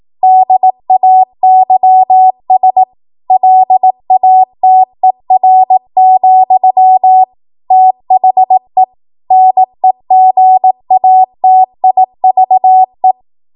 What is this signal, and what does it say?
Morse code, heard as a keyed tone, about 18 wpm, 750 Hz. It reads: DAYS LATER, THE NEGATIVE
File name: Morse / 18WPM